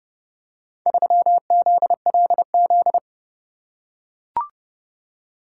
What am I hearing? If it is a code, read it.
Morse code, heard as a keyed tone, about 28 wpm, 700 Hz. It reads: 3ZLZ